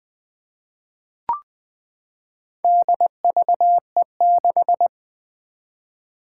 Morse code, heard as a keyed tone, about 20 wpm, 700 Hz. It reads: DVE6
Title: Four Challenging Letter-Number Combinations 20wpm